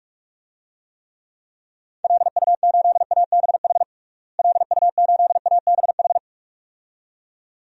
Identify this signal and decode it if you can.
Morse code, heard as a keyed tone, about 47 wpm, 700 Hz. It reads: LU8ABH LU8ABH
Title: DXCC-100 Call Signs - 6 Characters - Repeated 2x - 45wpm